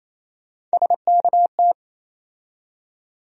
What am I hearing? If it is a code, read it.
Morse code, heard as a keyed tone, about 28 wpm, 700 Hz. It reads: SKT
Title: USA Call Sign Postfixes - 3 Letters 28wpm